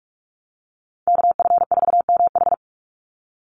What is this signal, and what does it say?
Morse code, heard as a keyed tone, about 46 wpm, 700 Hz. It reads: KF4MH
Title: NA SOTA Chaser Callsigns 45wpm